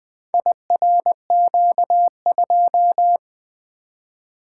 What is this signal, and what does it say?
Morse code, heard as a keyed tone, about 20 wpm, 700 Hz. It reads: IRQ2
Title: Four Letter-Number Combinations - 20wpm